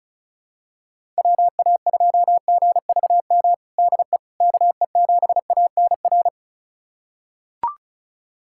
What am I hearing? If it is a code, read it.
Morse code, heard as a keyed tone, about 36 wpm, 700 Hz. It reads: WA2GVM DE KE7ANR
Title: USA Callsign DE Callsign - 6 Characters - 35wpm